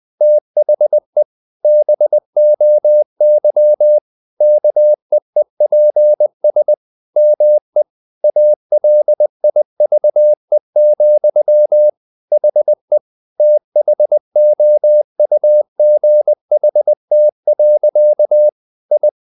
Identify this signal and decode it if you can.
Morse code, heard as a keyed tone, about 20 wpm, 600 Hz. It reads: THE BOY KEEPS ME ALIVE, HE THOUGHT. I
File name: the old man and the sea part 2 20wpm